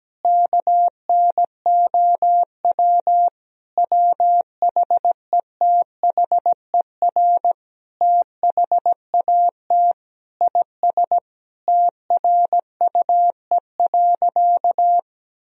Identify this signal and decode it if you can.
Morse code, heard as a keyed tone, about 17 wpm, 700 Hz. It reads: KNOW WHETHER THAT IS TRUE.
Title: Sentences from Top 1000 Words - Review Part 3 - 17wpm